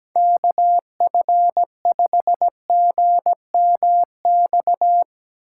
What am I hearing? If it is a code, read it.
Morse code, heard as a keyed tone, about 17 wpm, 700 Hz. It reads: KF5GMX
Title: USA Call Signs - 4x Character Spacing - 17wpm